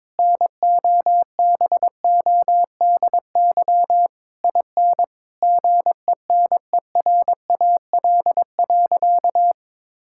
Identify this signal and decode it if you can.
Morse code, heard as a keyed tone, about 21 wpm, 700 Hz. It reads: NOBODY IN GENERAL.
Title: Sentences from Top 1600 Words 22wpm